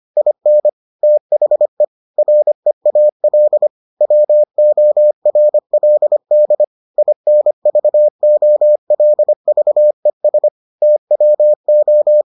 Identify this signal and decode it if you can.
Morse code, heard as a keyed tone, about 25 wpm, 600 Hz. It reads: IN THE REAL WORLD INVOLVES TWO